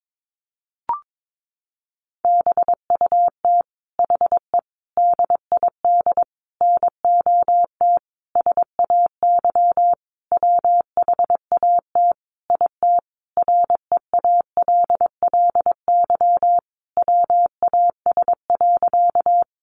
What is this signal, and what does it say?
Morse code, heard as a keyed tone, about 22 wpm, 700 Hz. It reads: BUT HE DID NOT SAY WHAT IT REALLY WAS.
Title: Sentences from Top 600 Words 22wpm